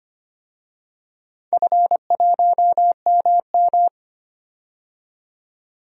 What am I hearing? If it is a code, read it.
Morse code, heard as a keyed tone, about 25 wpm, 700 Hz. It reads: F1MM
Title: DXCC-100 Call Signs - 4 Characters - 25wpm